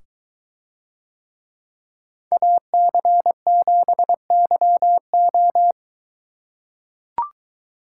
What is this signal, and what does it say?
Morse code, heard as a keyed tone, about 22 wpm, 700 Hz. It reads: AC7YO